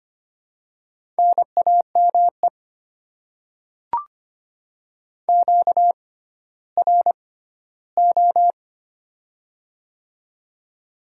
Morse code, heard as a keyed tone, about 25 wpm, spaced as for 14 wpm, 700 Hz. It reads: NAME QRO
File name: Top 90 QSO Elements - 6x Character Spacing - 25wpm